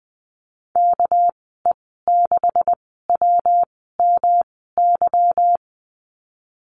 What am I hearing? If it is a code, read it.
Morse code, heard as a keyed tone, about 19 wpm, spaced as for 14 wpm, 700 Hz. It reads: KE6WMY
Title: 200 Lesson 32g - Letter k - Call Signs 20wpm